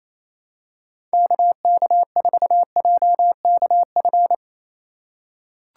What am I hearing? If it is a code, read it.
Morse code, heard as a keyed tone, about 28 wpm, 700 Hz. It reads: KK4JKF
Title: USA Call Signs - 6 Characters - 4x Character Spacing - 28wpm